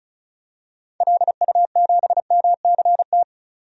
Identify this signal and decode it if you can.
Morse code, heard as a keyed tone, about 34 wpm, 700 Hz. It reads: LU7MCT
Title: DXCC-100 Call Signs - 6 Characters - 35wpm